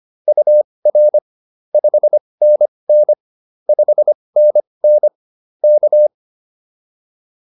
Morse code, heard as a keyed tone, about 25 wpm, spaced as for 20 wpm, 590 Hz. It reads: UR 5NN 5NN K